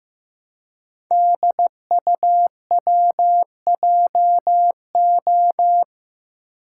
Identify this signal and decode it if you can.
Morse code, heard as a keyed tone, about 15 wpm, 700 Hz. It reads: DUWJO
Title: Code Groups - 5 Letter:Numbers - 15wpm